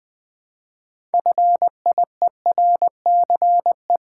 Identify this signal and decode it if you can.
Morse code, heard as a keyed tone, about 20 wpm, 700 Hz. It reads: FIERCE
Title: 6 Letter Words - First 3 Letters Given - 20wpm